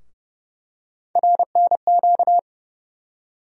Morse code, heard as a keyed tone, about 30 wpm, 700 Hz. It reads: RNQ
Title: USA Call Sign Postfixes - 3 Letters 30wpm